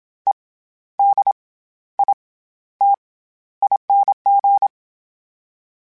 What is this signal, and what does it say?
Morse code, heard as a keyed tone, about 27 wpm, 800 Hz. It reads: E D I T ING
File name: Special INGs-mp3s / Special INGs-mp3s / SlowToFaster ing @25